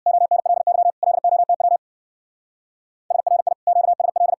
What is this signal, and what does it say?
Morse code, heard as a keyed tone, about 61 wpm, 700 Hz. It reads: ONLY LOTW SRI QSL
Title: Speed-Racing Common QSO Phrases - 50wpm